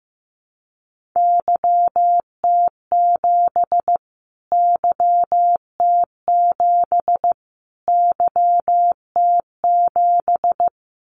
Morse code, heard as a keyed tone, about 15 wpm, 700 Hz. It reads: YT7 YT7 YT7
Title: Callsign Prefixes for Top 25 DXCC - Repeated 3x - 15wpm